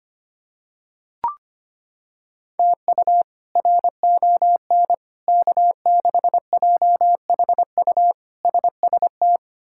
Morse code, heard as a keyed tone, about 25 wpm, 700 Hz. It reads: TU RON K6JHU SST
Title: K1USN SST QSO Practice - 25wpm